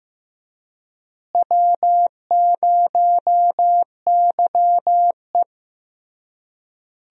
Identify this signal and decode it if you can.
Morse code, heard as a keyed tone, about 15 wpm, 700 Hz. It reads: W0YE
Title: USA Call Signs - 4 Characters - Repeated 3x - 15wpm